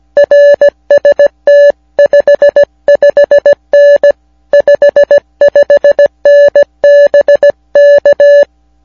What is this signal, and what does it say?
Morse code, heard as a keyed tone, about 17 wpm, 580 Hz. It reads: RST 55N 55N BK